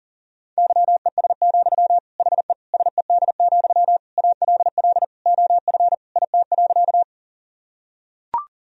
Morse code, heard as a keyed tone, about 37 wpm, 700 Hz. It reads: YES, HE SED, ALL OF IT.
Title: Sentences from Top 500 Words and Abbreviations 40wpm